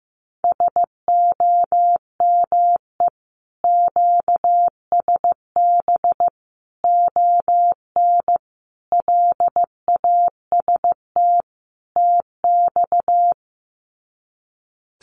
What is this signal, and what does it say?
Morse code, heard as a keyed tone, about 15 wpm, 700 Hz. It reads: SOME QSB ON LAST TX
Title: Common QSO Phrases 15wpm